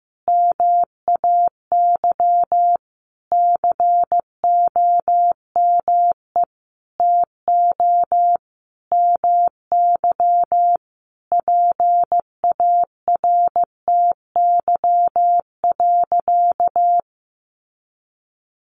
Morse code, heard as a keyed tone, about 15 wpm, 700 Hz. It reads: MAY COME TO MY PARTY.
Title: Sentences from Top 1400 Words 15wpm